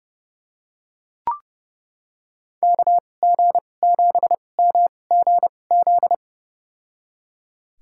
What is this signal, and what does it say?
Morse code, heard as a keyed tone, about 30 wpm, spaced as for 22 wpm, 700 Hz. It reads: KG7MGZ
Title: USA Call Signs - 2x Character Spacing - 30wpm